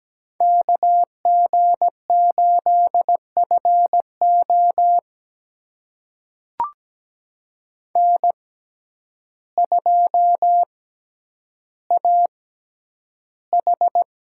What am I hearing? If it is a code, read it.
Morse code, heard as a keyed tone, about 17 wpm, spaced as for 9 wpm, 700 Hz. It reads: KG8FO N2AH
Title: USA Call Signs - 5 Characters - 6x Character Spacing - 17wpm